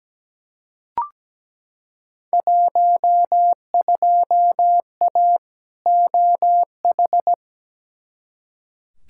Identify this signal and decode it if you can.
Morse code, heard as a keyed tone, about 17 wpm, 700 Hz. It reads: 12A OH
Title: Field Day Exchanges Encore - 17wpm